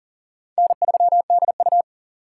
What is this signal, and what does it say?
Morse code, heard as a keyed tone, about 40 wpm, 700 Hz. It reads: N3DU
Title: USA Call Signs - 4 Characters - 6x Character Spacing - 40wpm